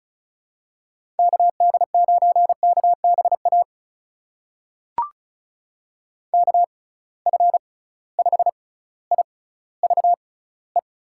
Morse code, heard as a keyed tone, about 36 wpm, spaced as for 18 wpm, 700 Hz. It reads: KD9KBA KF5IVE
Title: USA Call Signs - 6 Characters - 6x Character Spacing - 35wpm